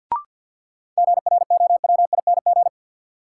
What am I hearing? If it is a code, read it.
Morse code, heard as a keyed tone, about 48 wpm, 700 Hz. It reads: GROWING
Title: Sets of 5 Words Encore - New Word Drill - 50wpm